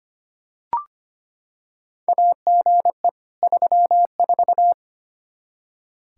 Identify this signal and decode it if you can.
Morse code, heard as a keyed tone, about 25 wpm, 700 Hz. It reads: AGE 34
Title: K1USN SST QSO Practice 25wpm